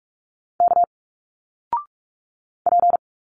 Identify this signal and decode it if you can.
Morse code, heard as a keyed tone, about 46 wpm, 700 Hz. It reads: K P
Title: Single Letters 45wpm